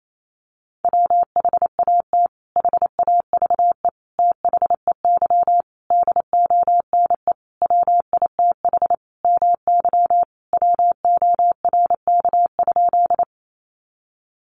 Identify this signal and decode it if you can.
Morse code, heard as a keyed tone, about 28 wpm, 700 Hz. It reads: WHAT HAVE THEY DONE WITH MY WORK?
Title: Sentences from Top 500 Words 28wpm